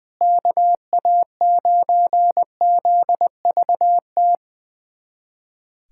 Morse code, heard as a keyed tone, about 21 wpm, 700 Hz. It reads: KA9ZVT
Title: USA Callsign DE Callsign - 6 Characters - 20wpm